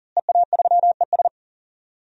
Morse code, heard as a keyed tone, about 40 wpm, 700 Hz. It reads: EA3ES
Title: DXCC-100 Call Signs - 5 Characters - Repeated 2x - 40wpm